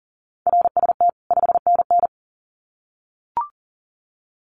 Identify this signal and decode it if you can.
Morse code, heard as a keyed tone, about 40 wpm, 700 Hz. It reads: RST 5NN
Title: Common QSO Phrases 40wpm